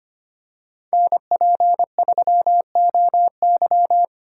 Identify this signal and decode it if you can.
Morse code, heard as a keyed tone, about 25 wpm, 700 Hz. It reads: NP3OY
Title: USA Call Signs - 5 Characters - 3x Character Spacing - 25wpm